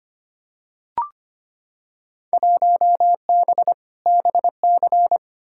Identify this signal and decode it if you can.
Morse code, heard as a keyed tone, about 25 wpm, 700 Hz. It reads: 1B BC
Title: Field Day Exchanges - 25wpm